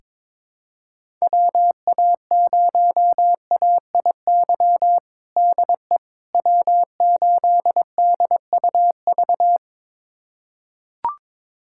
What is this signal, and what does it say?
Morse code, heard as a keyed tone, about 22 wpm, 700 Hz. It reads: WA0AIY DE W8DUV
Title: USA Callsign DE Callsign 22wpm